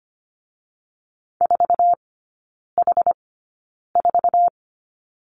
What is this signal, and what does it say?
Morse code, heard as a keyed tone, about 25 wpm, spaced as for 9 wpm, 690 Hz. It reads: 4H4
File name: Lesson 14e - Number 2 - Three Characters 25wpm